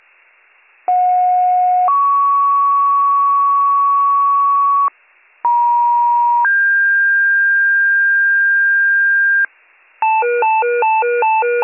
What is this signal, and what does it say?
Two-tone page, heard as a one-second tone, then a three-second tone, about 730 then 1120 Hz, about 950 then 1600 Hz, with noise. It is heard from 0.9 to 9.5 s.